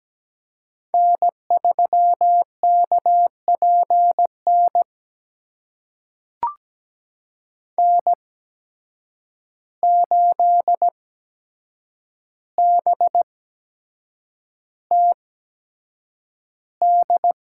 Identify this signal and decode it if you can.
Morse code, heard as a keyed tone, about 17 wpm, spaced as for 7 wpm, 700 Hz. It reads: N3KPN N8BTD
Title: USA Call Signs - 5 Characters - 8x Character Spacing - 17wpm